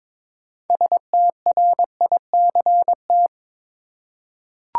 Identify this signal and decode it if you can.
Morse code, heard as a keyed tone, about 21 wpm, 700 Hz. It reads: STRICT
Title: Six Letter Words - 22wpm